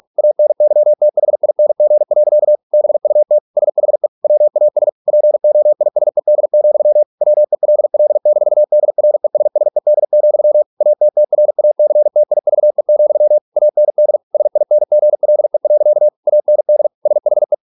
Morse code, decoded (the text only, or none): ANYTHING. BUT SHE WAS POISED, WELL-DRESSED, ATTRACTIVE, AND SINGLE. AND SHE